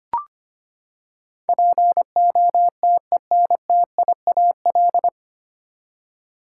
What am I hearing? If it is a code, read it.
Morse code, heard as a keyed tone, about 25 wpm, 700 Hz. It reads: POTENTIAL
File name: Nine Letter Words - 25wpm